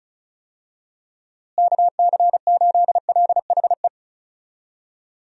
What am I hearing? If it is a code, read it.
Morse code, heard as a keyed tone, about 36 wpm, 700 Hz. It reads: KC8LHE
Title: USA Call Signs - 6 Characters - Repeated 2x - 35wpm